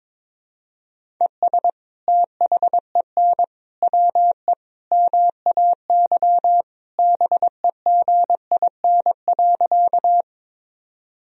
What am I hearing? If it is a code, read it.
Morse code, heard as a keyed tone, about 22 wpm, 700 Hz. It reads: ES THEN WE MAY BEGIN.
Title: Sentences from Top 300 Words and Abbreviations 22wpm